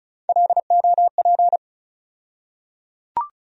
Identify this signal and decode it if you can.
Morse code, heard as a keyed tone, about 36 wpm, 700 Hz. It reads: LOP
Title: USA Call Sign Postfixes - 3 Letters 35wpm